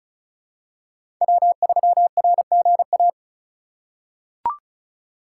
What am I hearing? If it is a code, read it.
Morse code, heard as a keyed tone, about 36 wpm, 700 Hz. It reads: W3RGA
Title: K1USN SST QSO Practice - 35wpm